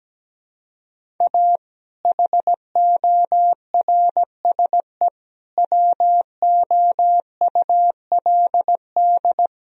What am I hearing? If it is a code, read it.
Morse code, heard as a keyed tone, about 17 wpm, 700 Hz. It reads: A HORSE WOULD
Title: Sentences from Top 1000 Words - Review Part 1 - 17wpm